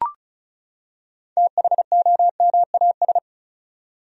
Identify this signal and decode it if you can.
Morse code, heard as a keyed tone, about 36 wpm, 700 Hz. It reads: THOMAS
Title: USA Common Names - 35wpm